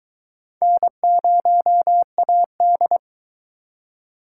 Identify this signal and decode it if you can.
Morse code, heard as a keyed tone, about 24 wpm, 700 Hz. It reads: N0AD